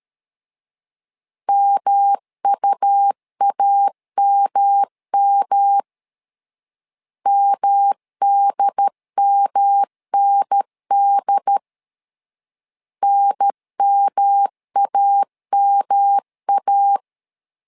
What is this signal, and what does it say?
Morse code, heard as a keyed tone, about 13 wpm, 800 Hz. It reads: MUAMM MDMND NMAMA
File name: LEZIONI / 3 MWRO / 3 M 12wpm 1541